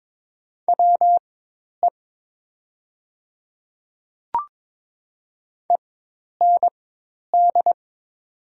Morse code, heard as a keyed tone, about 21 wpm, spaced as for 10 wpm, 700 Hz. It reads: WE END
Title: 4x Character Spacing - Top 200 Words - 22wpm